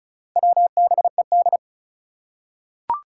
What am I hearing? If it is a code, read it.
Morse code, heard as a keyed tone, about 36 wpm, 690 Hz. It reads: WBED
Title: Four Challenging Letter-Number Combinations 35wpm